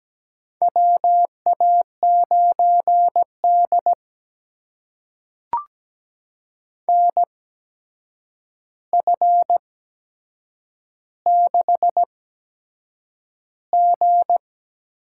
Morse code, heard as a keyed tone, about 17 wpm, spaced as for 7 wpm, 700 Hz. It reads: WA9D NF6G
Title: USA Call Signs - 4 Characters - 8x Character Spacing - 17wpm